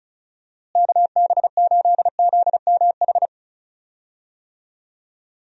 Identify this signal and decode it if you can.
Morse code, heard as a keyed tone, about 36 wpm, 700 Hz. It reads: KB8ZMH
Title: USA Call Signs - 6 Characters - Repeated 3x - 35wpm